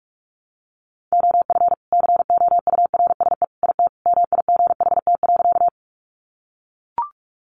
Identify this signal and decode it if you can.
Morse code, heard as a keyed tone, about 44 wpm, 700 Hz. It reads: OF COURSE IT MIGHT.
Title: Sentences from Top 500 Words 45wpm